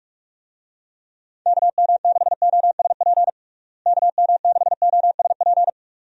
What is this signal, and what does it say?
Morse code, heard as a keyed tone, about 46 wpm, 700 Hz. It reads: KM6OSP KM6OSP
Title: USA Call Signs - 6 Characters - Repeated 2x - 45wpm